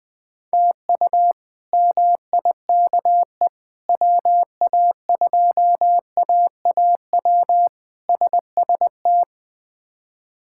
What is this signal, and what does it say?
Morse code, heard as a keyed tone, about 20 wpm, 700 Hz. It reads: TU MIKE WA2AAW SST
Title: K1USN SST QSO Practice - 20wpm